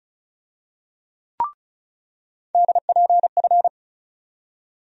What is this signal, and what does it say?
Morse code, heard as a keyed tone, about 35 wpm, 700 Hz. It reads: DPF
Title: USA Call Sign Postfixes - 3 Letters 35wpm